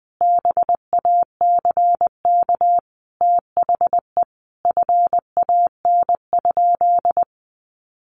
Morse code, heard as a keyed tone, about 20 wpm, 700 Hz. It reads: BACK THE FAN?
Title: Sentences from Top 1300 Words 20wpm